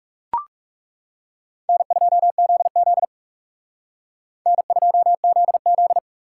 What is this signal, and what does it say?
Morse code, heard as a keyed tone, about 34 wpm, 700 Hz. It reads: N2ZZ N2ZZ
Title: Speed Racing - USA 1x2 Call Signs 30wpm